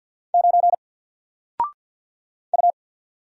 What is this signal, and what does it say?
Morse code, heard as a keyed tone, about 50 wpm, 700 Hz. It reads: 9 U
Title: Single Letter-Number 50wpm